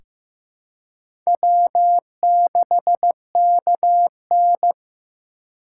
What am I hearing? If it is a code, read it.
Morse code, heard as a keyed tone, about 15 wpm, 700 Hz. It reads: W6KN
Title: USA Call Signs - 4 Characters - 4x Character Spacing - 15wpm